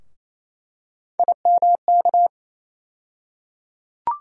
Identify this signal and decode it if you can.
Morse code, heard as a keyed tone, about 28 wpm, 700 Hz. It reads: IMK
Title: USA Call Sign Postfixes - 3 Letters 28wpm